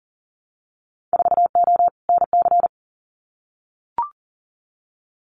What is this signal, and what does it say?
Morse code, heard as a keyed tone, about 40 wpm, 700 Hz. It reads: 4O NC
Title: Winter Field Day Exchanges 40wpm